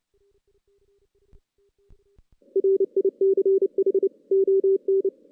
Morse code, heard as a keyed tone, about 29 wpm, 390 Hz. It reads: RICH ON
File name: 2016 JAN CW NAQP / 2016 Jan naqp k5ot - 0-5